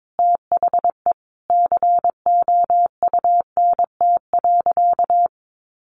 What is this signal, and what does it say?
Morse code, heard as a keyed tone, about 22 wpm, 700 Hz. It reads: THE COUNT.
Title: Sentences from Top 1500 Words 22wpm